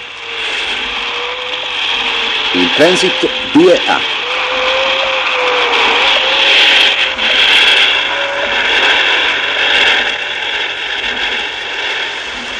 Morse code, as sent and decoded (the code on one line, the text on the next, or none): none